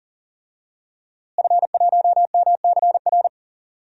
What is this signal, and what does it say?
Morse code, heard as a keyed tone, about 39 wpm, 700 Hz. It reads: F1MCR
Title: DXCC-100 Call Signs - 5 Characters - Repeated 2x - 40wpm